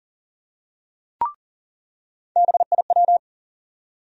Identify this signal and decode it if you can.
Morse code, heard as a keyed tone, about 40 wpm, 700 Hz. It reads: BIW
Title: USA Call Sign Postfixes - 3 Letters 40wpm